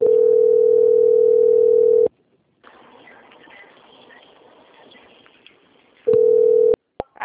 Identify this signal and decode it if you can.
Ringback tone (North America), 440 + 480 Hz.
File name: out- - - - - .8466